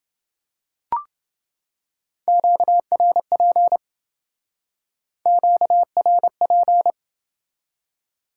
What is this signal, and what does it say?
Morse code, heard as a keyed tone, about 28 wpm, 700 Hz. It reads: QRP QRP